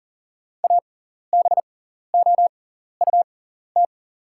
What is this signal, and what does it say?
Morse code, heard as a keyed tone, about 43 wpm, spaced as for 14 wpm, 700 Hz. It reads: ABOUT
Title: 6x Character Spacing - Top 200 Words - 40wpm